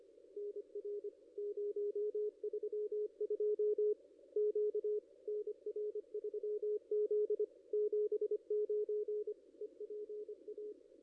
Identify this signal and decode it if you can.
Morse code, heard as a keyed tone, about 25 wpm, 440 Hz, with radio noise. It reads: NR 032 Q NR3Z 79 EPA
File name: contests / ss-cw-k5tr-op-n5rz -1500